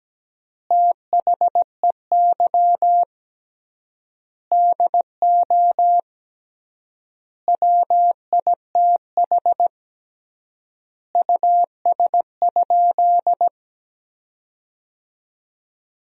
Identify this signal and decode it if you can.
Morse code, heard as a keyed tone, about 17 wpm, 700 Hz. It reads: THEY DO WITH US?
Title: 3x Word-Spacing - Sentences from Top 200 Words - 17wpm